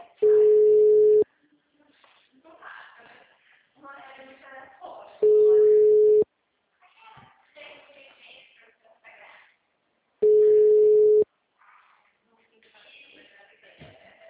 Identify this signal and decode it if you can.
Ringback tone (Europe, 425 Hz), with speech.
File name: out- -2012- - - .319